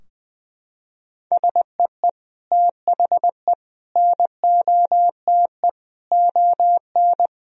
Morse code, heard as a keyed tone, about 20 wpm, 700 Hz. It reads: SEE THE NOTE ON
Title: Sentences from Top 1000 Words - Review Part 2 - 20wpm